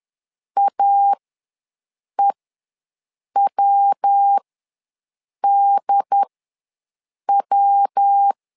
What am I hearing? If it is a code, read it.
Morse code, heard as a keyed tone, about 10 wpm, spaced as for 6 wpm, 800 Hz. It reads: AEWDW
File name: LEZIONI / 3 MWRO / 3 MW ws 10wpm 2213